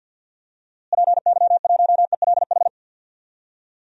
Morse code, heard as a keyed tone, about 52 wpm, 700 Hz. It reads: PY1ELH
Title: DXCC-100 Call Signs - 6 Characters - Repeated 3x - 50wpm